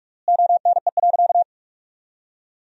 Morse code, heard as a keyed tone, about 46 wpm, 700 Hz. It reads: ONE.
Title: 3x Word-Spacing - Sentences from Top 300 Words - 45wpm